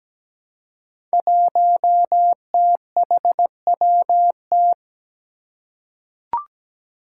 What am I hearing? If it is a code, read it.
Morse code, heard as a keyed tone, about 17 wpm, 700 Hz. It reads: 1THWT
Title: Five Challenging Letter-Number Combinations 17wpm